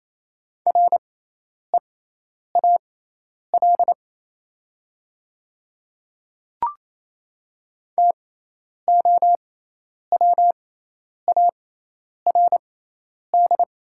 Morse code, heard as a keyed tone, about 28 wpm, spaced as for 10 wpm, 700 Hz. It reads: REAL TOWARD
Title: 6x Character Spacing - Top 500 Words - 28wpm